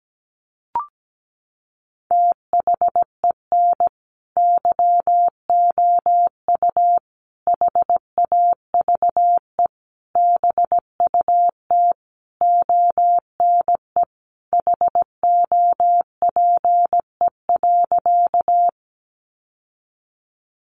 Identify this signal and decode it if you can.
Morse code, heard as a keyed tone, about 17 wpm, 700 Hz. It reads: THEN YOU HAVE BUT ONE HOPE.
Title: Sentences from Top 1000 Words 17wpm